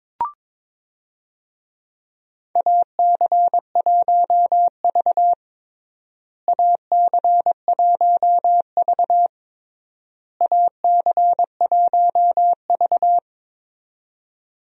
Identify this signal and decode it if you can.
Morse code, heard as a keyed tone, about 22 wpm, 700 Hz. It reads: AC1V AC1V AC1V
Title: USA Call Signs - 4 Characters - Repeated 3x - 22wpm